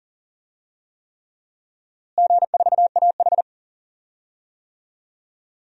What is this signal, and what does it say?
Morse code, heard as a keyed tone, about 40 wpm, 700 Hz. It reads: G4AH
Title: DXCC-100 Call Signs - 4 Characters - Repeated 3x - 40wpm